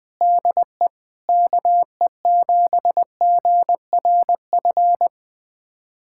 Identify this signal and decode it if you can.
Morse code, heard as a keyed tone, about 20 wpm, 700 Hz. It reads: DE KE7GRF
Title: USA Callsign DE Callsign 20wpm